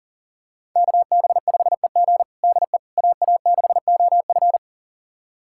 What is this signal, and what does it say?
Morse code, heard as a keyed tone, about 40 wpm, 700 Hz. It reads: KB5EG DE AA6OF
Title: USA Callsign DE Callsign - 5 Characters - 40wpm